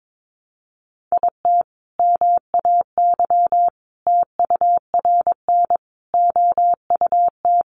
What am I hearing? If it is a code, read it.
Morse code, heard as a keyed tone, about 22 wpm, 700 Hz. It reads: IT MAY TURN OUT
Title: Sentences from Top 400 Words 22wpm